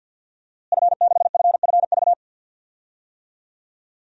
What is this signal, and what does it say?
Morse code, heard as a keyed tone, about 51 wpm, 700 Hz. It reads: F6FFV
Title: DXCC-100 Call Signs - 5 Characters - Repeated 2x - 50wpm